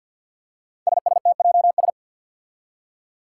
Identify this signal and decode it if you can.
Morse code, heard as a keyed tone, about 51 wpm, 700 Hz. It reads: SSTJS
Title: Five Challenging Letter-Number Combinations 50wpm